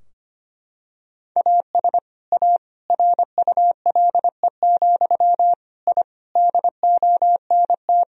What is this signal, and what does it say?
Morse code, heard as a keyed tone, about 25 wpm, 700 Hz. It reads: AS A RULE, I DONT
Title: Sentences from Top 1000 Words - Review Part 2 - 25wpm